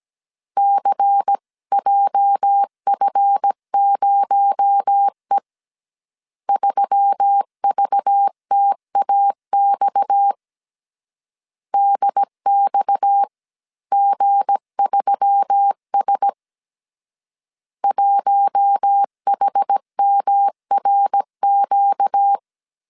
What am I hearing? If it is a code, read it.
Morse code, heard as a keyed tone, about 17 wpm, 800 Hz. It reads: CJF0E 3VTAX DXG3S 1HMRQ
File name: CW / DET / 1 9 1500C S C 16wpm 2909